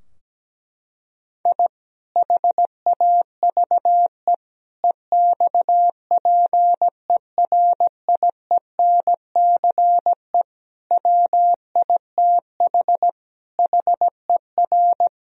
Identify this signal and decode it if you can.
Morse code, heard as a keyed tone, about 17 wpm, 700 Hz. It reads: I HAVE EXPERIENCE WITH HER
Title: Sentences from Top 1000 Words - Review Part 3 - 17wpm